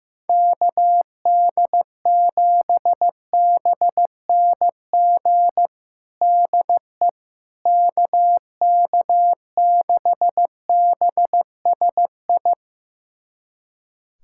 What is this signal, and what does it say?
Morse code, heard as a keyed tone, about 16 wpm, 700 Hz. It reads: KD7BNG DE KK6BSI
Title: USA Callsign DE Callsign - 6 Characters - 15wpm